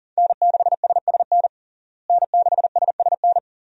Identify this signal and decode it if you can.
Morse code, heard as a keyed tone, about 37 wpm, 700 Hz. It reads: N6SSN N6SSN
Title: USA Call Signs - 5 Characters - Repeated 3x - 40wpm